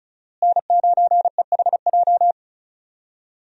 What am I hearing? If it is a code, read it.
Morse code, heard as a keyed tone, about 34 wpm, 700 Hz. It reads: N9EHJ
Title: USA Call Signs - 5 Characters - Repeated 2x - 35wpm